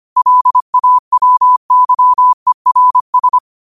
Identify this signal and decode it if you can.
Morse code, heard as a keyed tone, about 25 wpm, 1000 Hz. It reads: LAWYERS
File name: qotdfiles / 25wpm